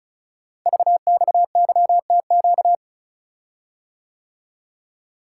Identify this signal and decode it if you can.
Morse code, heard as a keyed tone, about 36 wpm, 690 Hz. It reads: VXYTQ
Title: Code Groups - 5 Letter:Numbers - 35wpm